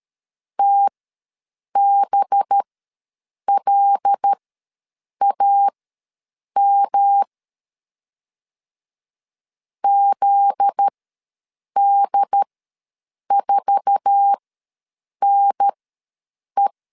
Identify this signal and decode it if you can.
Morse code, heard as a keyed tone, about 13 wpm, spaced as for 7 wpm, 800 Hz. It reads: TBLAM ZD4NE
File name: LEZIONI / 8 YJ45 / 8 YJ4 ws 12wpm 3249